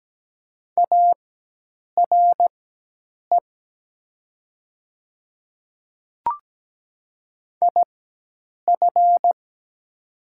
Morse code, heard as a keyed tone, about 17 wpm, spaced as for 8 wpm, 700 Hz. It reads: ARE IF
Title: 4x Character Spacing - Top 100 Words - 17wpm